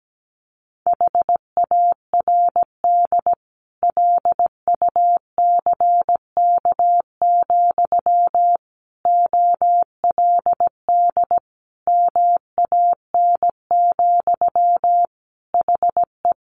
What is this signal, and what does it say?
Morse code, heard as a keyed tone, about 17 wpm, 700 Hz. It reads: HARD LUCK, OLD MAN, HE
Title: Sentences from Top 1500 Words 17wpm